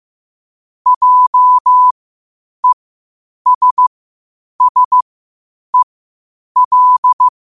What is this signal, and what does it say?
Morse code, heard as a keyed tone, about 15 wpm, spaced as for 8 wpm, 1000 Hz. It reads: JESSEL
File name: qotdfiles / 8wpm